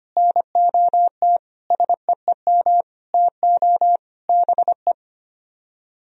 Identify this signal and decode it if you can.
Morse code, heard as a keyed tone, about 33 wpm, 700 Hz. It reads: NOT SEEM TO BE